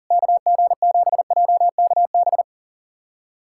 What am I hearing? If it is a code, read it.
Morse code, heard as a keyed tone, about 37 wpm, 700 Hz. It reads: KG7JKB
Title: USA Callsign DE Callsign 40wpm